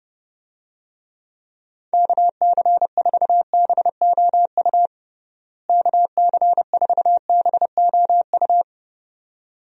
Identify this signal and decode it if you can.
Morse code, heard as a keyed tone, about 30 wpm, 700 Hz. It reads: KC4BOU KC4BOU
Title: USA Call Signs - 6 Characters - Repeated 2x - 30wpm